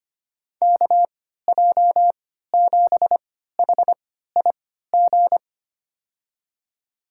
Morse code, heard as a keyed tone, about 25 wpm, spaced as for 14 wpm, 700 Hz. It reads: KJ7HIG
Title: USA Call Signs - 3x Character Spacing - 25wpm